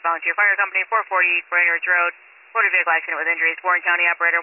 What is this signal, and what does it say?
Emergency Alert System (SAME) tones, heard as harsh buzzing bursts, with speech. It is heard from 0.2 to 3.5 s.